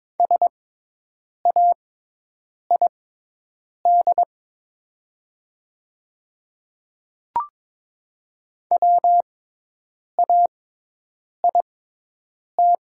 Morse code, heard as a keyed tone, about 22 wpm, spaced as for 8 wpm, 700 Hz. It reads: SAID WAIT
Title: 6x Character Spacing - Top 500 Words - 22wpm